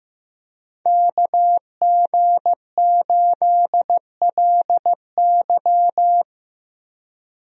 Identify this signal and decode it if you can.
Morse code, heard as a keyed tone, about 15 wpm, 700 Hz. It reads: KG8LY
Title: USA Call Signs - 5 Characters - Repeated 3x - 15wpm